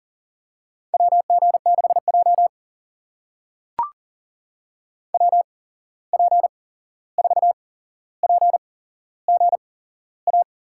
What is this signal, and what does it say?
Morse code, heard as a keyed tone, about 40 wpm, spaced as for 15 wpm, 700 Hz. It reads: WG6J WP4PGA
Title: USA Call Signs - 8x Character Spacing - 40wpm